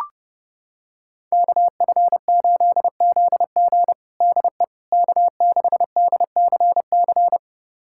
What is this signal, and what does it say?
Morse code, heard as a keyed tone, about 31 wpm, 700 Hz. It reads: KF8ZG DE K6DCC
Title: USA Callsign DE Callsign - 5 Characters - 30wpm